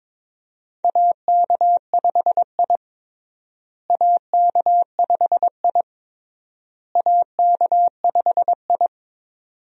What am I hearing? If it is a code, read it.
Morse code, heard as a keyed tone, about 22 wpm, 700 Hz. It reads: AK5I AK5I AK5I
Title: USA Call Signs - 4 Characters - Repeated 3x - 22wpm